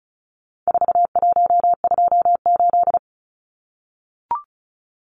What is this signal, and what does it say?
Morse code, heard as a keyed tone, about 34 wpm, 700 Hz. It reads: 4128
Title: Numbers - 4-Digit 35wpm